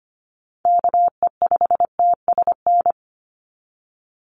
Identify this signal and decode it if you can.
Morse code, heard as a keyed tone, about 25 wpm, 700 Hz. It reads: KE5TSN
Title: USA Call Signs 25wpm